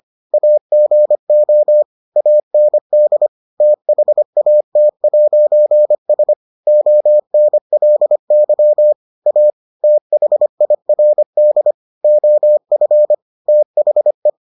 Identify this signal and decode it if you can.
Morse code, heard as a keyed tone, about 25 wpm, 600 Hz. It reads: AGO AND THAT'S ONLY A THIRD OF THE